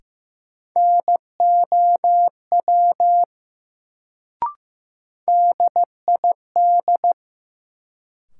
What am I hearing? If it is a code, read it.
Morse code, heard as a keyed tone, about 15 wpm, 700 Hz. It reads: NOW DID
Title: Sentences from Top 100 Words - New Word Drill - 15wpm